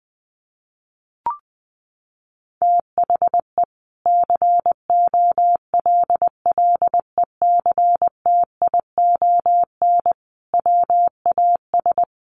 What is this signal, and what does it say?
Morse code, heard as a keyed tone, about 20 wpm, 700 Hz. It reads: THE COLLECTION WAS
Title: Sentences from Top 1200 Words 20wpm